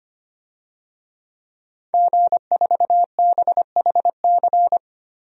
Morse code, heard as a keyed tone, about 25 wpm, 700 Hz. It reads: G4BHC
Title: DXCC-100 Call Signs - 5 Characters - 25wpm